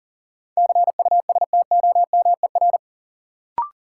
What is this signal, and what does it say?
Morse code, heard as a keyed tone, about 38 wpm, 700 Hz. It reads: CUSTOMER
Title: Sentences from Top 1100 Words - New Word Drill - 40wpm